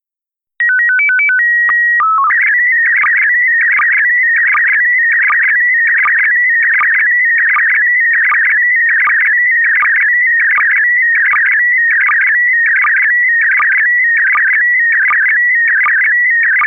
SSTV (slow-scan television), PD 180.